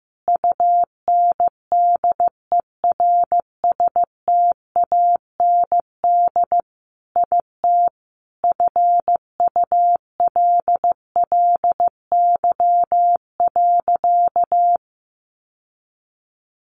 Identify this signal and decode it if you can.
Morse code, heard as a keyed tone, about 15 wpm, 700 Hz. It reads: UNDERSTAND IT FULLY.
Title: Sentences from Top 1300 Words 15wpm